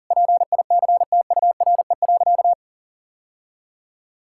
Morse code, heard as a keyed tone, about 40 wpm, 700 Hz. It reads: PICTURE.
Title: Sentences from Top 1000 Words - Review Part 6 - 40wpm